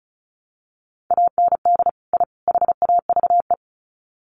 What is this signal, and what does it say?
Morse code, heard as a keyed tone, about 34 wpm, 700 Hz. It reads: AND I HAVE
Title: Sets of 3 Words 35wpm